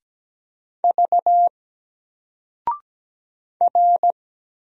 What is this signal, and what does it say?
Morse code, heard as a keyed tone, about 17 wpm, 700 Hz. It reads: V R
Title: Single Character (Letter:Number:Punctuation:Prosign) Alt 17wpm